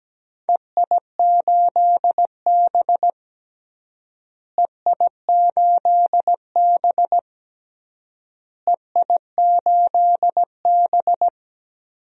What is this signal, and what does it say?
Morse code, heard as a keyed tone, about 17 wpm, 700 Hz. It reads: EI8B EI8B EI8B
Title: DXCC-100 Call Signs - 4 Characters - Repeated 3x - 17wpm